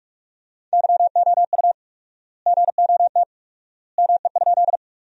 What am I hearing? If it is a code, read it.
Morse code, heard as a keyed tone, about 47 wpm, 700 Hz. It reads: YOU GOT ME?
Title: 4x Word-Spacing - Sentences from Top 300 Words - 45wpm